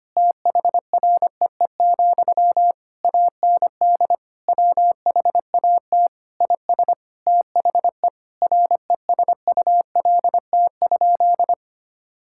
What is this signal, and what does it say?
Morse code, heard as a keyed tone, about 25 wpm, 700 Hz. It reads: THREE, AND WHAT IS THE RESULT?
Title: Sentences from Top 1600 Words 25wpm